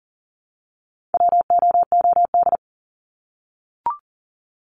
Morse code, heard as a keyed tone, about 40 wpm, 700 Hz. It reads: WOOD
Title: Top 300 Words 40wpm